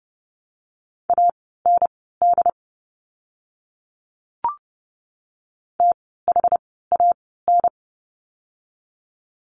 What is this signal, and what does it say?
Morse code, heard as a keyed tone, about 30 wpm, spaced as for 17 wpm, 700 Hz. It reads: AND THAN
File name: course / 30:10wpm / 237 Lesson 38f - Letter q - Words 30wpm